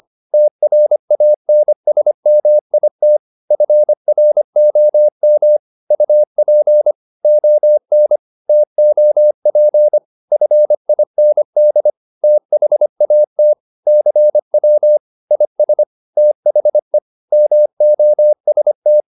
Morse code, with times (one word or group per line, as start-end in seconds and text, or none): 0.3-3.2 TRANSMIT
3.5-5.6 FROM
5.9-6.9 UP
7.2-8.2 ON
8.5-10.0 TOP
10.3-11.9 FIND
12.2-13.5 THAT
13.9-15.0 CW
15.3-15.8 IS
16.2-17.0 THE
17.3-19.0 MOST